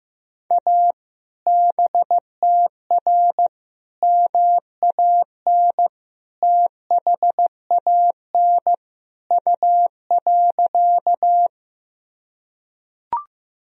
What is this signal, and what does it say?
Morse code, heard as a keyed tone, about 15 wpm, 700 Hz. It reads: A BTR MAN THAN U.
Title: Sentences from Top 400 Words and Abbreviations 15wpm